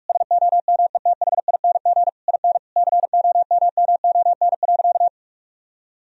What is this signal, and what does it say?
Morse code, heard as a keyed tone, about 46 wpm, 700 Hz. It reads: SOMETHING IN COMMON.
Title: Sentences from Top 1000 Words - Review Part 3 - 45wpm